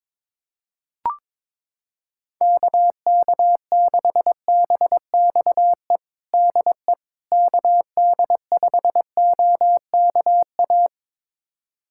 Morse code, heard as a keyed tone, about 22 wpm, 700 Hz. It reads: KK6BXE DE KD5OKA
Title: USA Callsign DE Callsign - 6 Characters - 22wpm